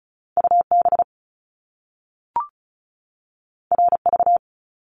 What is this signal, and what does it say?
Morse code, heard as a keyed tone, about 36 wpm, 700 Hz. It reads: UB RV
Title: Top Two Letter Combinations in Words 35wpm